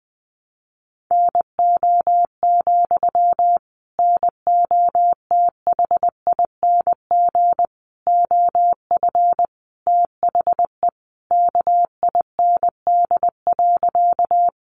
Morse code, heard as a keyed tone, about 20 wpm, 700 Hz. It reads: NO, NOTHING OF THE KIND.
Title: Sentences from Top 500 Words 20wpm